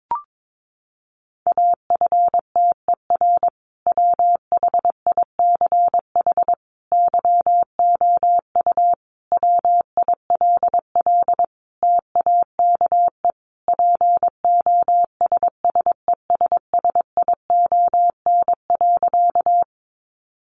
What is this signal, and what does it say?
Morse code, heard as a keyed tone, about 22 wpm, 700 Hz. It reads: AFTER WHICH YOU WILL TAKE POSSESSION.
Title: Sentences from Top 1400 Words 22wpm